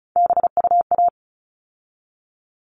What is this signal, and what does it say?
Morse code, heard as a keyed tone, about 34 wpm, 700 Hz. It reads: BUA
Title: All Cut Numbers - 3-Digit 35wpm